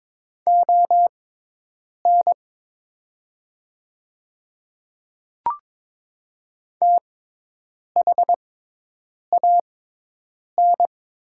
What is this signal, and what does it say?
Morse code, heard as a keyed tone, about 22 wpm, spaced as for 8 wpm, 700 Hz. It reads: ON THAN
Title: 6x Character Spacing - Top 100 Words - 22wpm